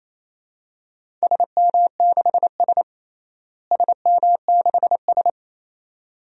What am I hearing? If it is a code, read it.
Morse code, heard as a keyed tone, about 28 wpm, 700 Hz. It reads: SM6S SM6S
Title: DXCC-100 Call Signs - 4 Characters - Repeated 2x - 28wpm